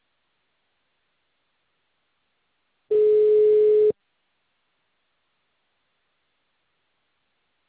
Ringback tone (Europe, 425 Hz).